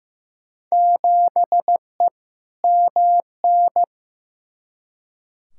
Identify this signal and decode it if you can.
Morse code, heard as a keyed tone, about 15 wpm, 700 Hz. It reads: 7E MN
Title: Field Day Exchanges Encore - 15wpm